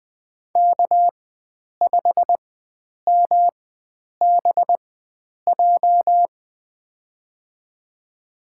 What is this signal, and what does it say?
Morse code, heard as a keyed tone, about 20 wpm, spaced as for 9 wpm, 700 Hz. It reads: K5MBJ
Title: USA Call Signs - 5 Characters - 4x Character Spacing - 20wpm